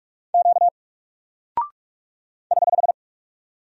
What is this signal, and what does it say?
Morse code, heard as a keyed tone, about 44 wpm, 700 Hz. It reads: Q <HH>
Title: Single Character (Letter:Number:Punctuation:Prosign) Alt 45wpm